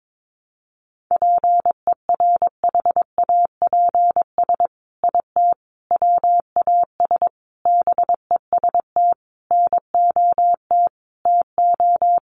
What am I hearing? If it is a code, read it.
Morse code, heard as a keyed tone, about 22 wpm, 700 Hz. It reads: PERHAPS IT WAS BEST NOT TO
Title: Sentences from Top 600 Words 22wpm